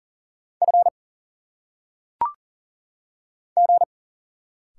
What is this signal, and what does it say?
Morse code, heard as a keyed tone, about 38 wpm, 700 Hz. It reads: F G
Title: Single Letters - Phonetic Pronunciation 40wpm